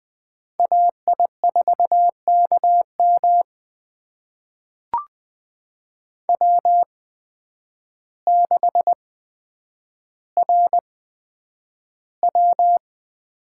Morse code, heard as a keyed tone, about 20 wpm, spaced as for 9 wpm, 700 Hz. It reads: AI4KM W6RW
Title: USA Call Signs - 8x Character Spacing - 20wpm